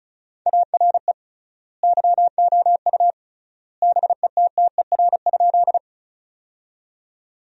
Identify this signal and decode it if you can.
Morse code, heard as a keyed tone, about 36 wpm, 700 Hz. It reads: ARE YOU BETTER?
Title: 3x Word-Spacing - Sentences from Top 400 Words - 35wpm